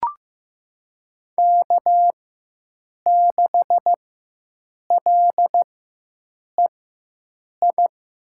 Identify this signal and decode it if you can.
Morse code, heard as a keyed tone, about 15 wpm, spaced as for 7 wpm, 700 Hz. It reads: K6LEI
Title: USA Call Signs - 5 Characters - 4x Character Spacing - 15wpm